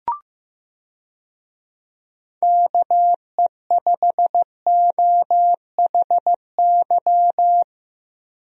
Morse code, heard as a keyed tone, about 15 wpm, 700 Hz. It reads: KE5OHY
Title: USA Call Signs - 6 Characters - 15wpm